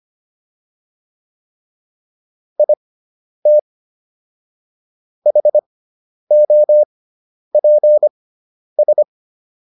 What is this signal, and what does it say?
Morse code, heard as a keyed tone, about 25 wpm, spaced as for 10 wpm, 600 Hz. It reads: IT HOPS